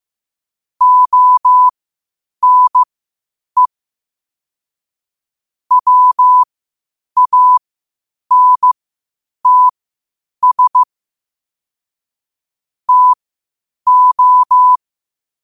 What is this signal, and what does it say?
Morse code, heard as a keyed tone, about 15 wpm, spaced as for 8 wpm, 1000 Hz. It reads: ONE WANTS TO
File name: qotdfiles / 8wpm